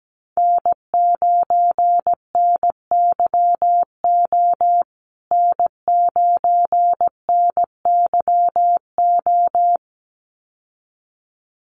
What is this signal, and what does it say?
Morse code, heard as a keyed tone, about 17 wpm, 700 Hz. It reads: N9NYO N9NYO
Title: USA Call Signs Repeated 2x 17wpm